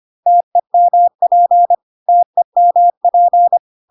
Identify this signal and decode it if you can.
Morse code, heard as a keyed tone, about 25 wpm, 700 Hz. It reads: TEMP TEMP
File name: cw / CW common abbreviations alphabetical repeated 3 times 25 25wpm ik1hge